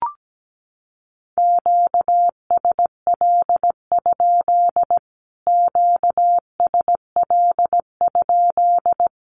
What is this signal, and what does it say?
Morse code, heard as a keyed tone, about 17 wpm, 700 Hz. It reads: QSL? QSL?
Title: Q-Codes Repeated 3x 17wpm